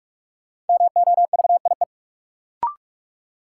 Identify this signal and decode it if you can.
Morse code, heard as a keyed tone, about 45 wpm, 700 Hz. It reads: MOVIE
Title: Sentences from Top 1000 Words - Review Part 5 - New Word Drill - 45wpm